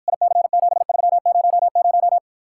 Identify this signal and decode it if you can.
Morse code, heard as a keyed tone, about 54 wpm, 700 Hz. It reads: IC7300